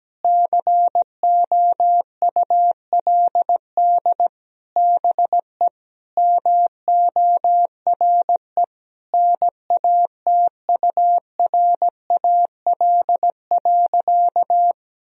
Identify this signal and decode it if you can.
Morse code, heard as a keyed tone, about 17 wpm, 700 Hz. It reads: COULD BE MORE NATURAL.
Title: Sentences from Top 1000 Words - Review Part 4 - 17wpm